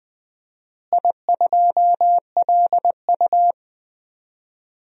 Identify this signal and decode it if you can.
Morse code, heard as a keyed tone, about 20 wpm, 700 Hz. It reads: I2LU
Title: DXCC-100 Call Signs - 4 Characters - Repeated 2x - 20wpm